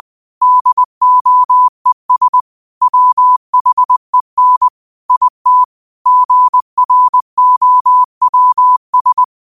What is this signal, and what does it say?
Morse code, heard as a keyed tone, about 20 wpm, 1000 Hz. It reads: DOES WHEN IT GROWS
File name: qotdfiles / 20wpm